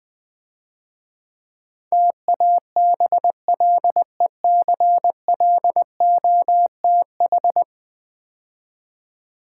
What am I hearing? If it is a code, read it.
Morse code, heard as a keyed tone, about 20 wpm, 700 Hz. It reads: TABLECLOTH
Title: Easy Words - Speed-Racing with Hints - 20wpm